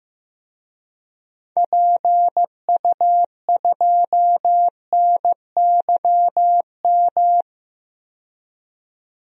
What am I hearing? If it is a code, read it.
Morse code, heard as a keyed tone, about 15 wpm, 700 Hz. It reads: PU2NYM
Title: DXCC-100 Call Signs - 6 Characters - 15wpm